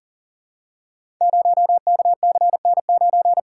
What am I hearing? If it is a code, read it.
Morse code, heard as a keyed tone, about 42 wpm, 700 Hz. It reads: 0KCN9
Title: Code Groups - 5 Letter:Numbers - 40wpm